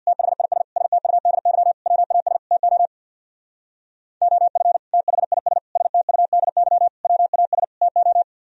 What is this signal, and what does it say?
Morse code, heard as a keyed tone, about 64 wpm, 700 Hz. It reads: THIS STUDY WAS TO OF THIS STUDY WAS TO